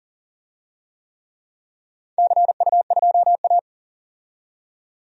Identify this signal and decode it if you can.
Morse code, heard as a keyed tone, about 40 wpm, 700 Hz. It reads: CU2A
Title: DXCC-100 Call Signs - 4 Characters - 40wpm